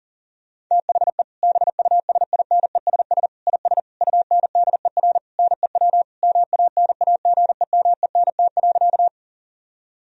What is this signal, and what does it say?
Morse code, heard as a keyed tone, about 40 wpm, 700 Hz. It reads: THE BUSINESS IS UNDER NEW MANAGEMENT.
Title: Sentences from Top 1000 Words - Review Part 4 - 40wpm